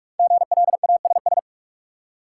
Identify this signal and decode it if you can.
Morse code, heard as a keyed tone, about 44 wpm, 700 Hz. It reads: GLASS
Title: Five Letter Words - 45wpm